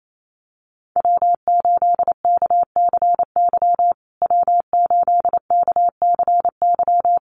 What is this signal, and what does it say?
Morse code, heard as a keyed tone, about 28 wpm, 700 Hz. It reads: W8KCY W8KCY
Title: USA Call Signs Repeated 3x 28wpm